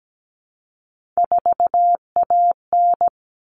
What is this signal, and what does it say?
Morse code, heard as a keyed tone, about 17 wpm, 700 Hz. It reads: 4AN
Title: All Cut Numbers - 3-Digit 17wpm